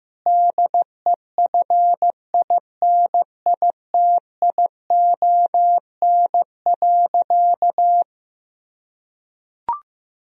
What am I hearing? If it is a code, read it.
Morse code, heard as a keyed tone, about 15 wpm, 700 Hz. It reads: DEFINITION.
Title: Sentences from Top 1000 Words - Review Part 4 - 15wpm